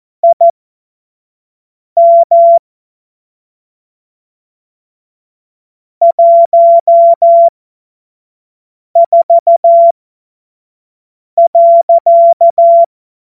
Morse code, heard as a keyed tone, about 14 wpm, spaced as for 5 wpm, 680 Hz. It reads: IM 14.
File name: Morsekurs / Witze / Witz 02 Januar 2024 25 F70